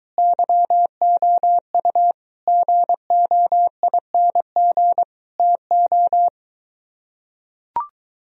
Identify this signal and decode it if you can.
Morse code, heard as a keyed tone, about 23 wpm, 700 Hz. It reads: YOU GOING TO